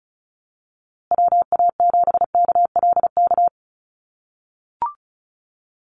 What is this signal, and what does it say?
Morse code, heard as a keyed tone, about 34 wpm, 700 Hz. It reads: WA7KLK
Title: USA Call Signs Repeated 3x 35wpm